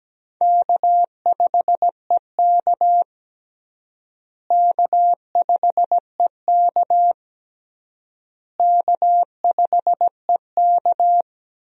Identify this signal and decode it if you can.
Morse code, heard as a keyed tone, about 17 wpm, 700 Hz. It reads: K5EK K5EK K5EK
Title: USA Call Signs - 4 Characters - Repeated 3x - 17wpm